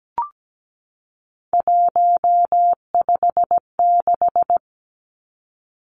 Morse code, heard as a keyed tone, about 17 wpm, 700 Hz. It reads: 156
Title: Numbers - 3-Digit 17wpm